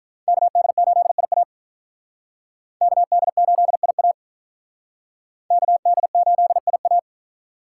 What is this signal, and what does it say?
Morse code, heard as a keyed tone, about 45 wpm, 700 Hz. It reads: KD8IA KD8IA KD8IA